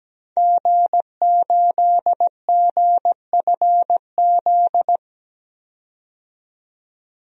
Morse code, heard as a keyed tone, about 17 wpm, 700 Hz. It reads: G8GFZ
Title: DXCC-100 Call Signs - 5 Characters - 17wpm